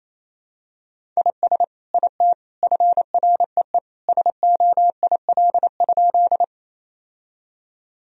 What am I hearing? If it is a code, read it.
Morse code, heard as a keyed tone, about 28 wpm, 700 Hz. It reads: IS IT FREE SOIL?
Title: Sentences from Top 1000 Words - Review Part 4 - 28wpm